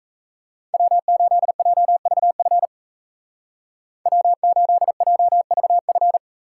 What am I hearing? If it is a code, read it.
Morse code, heard as a keyed tone, about 35 wpm, 700 Hz. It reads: W8JVF W8JVF
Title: Speed Racing - USA 1x3 Call Signs 28wpm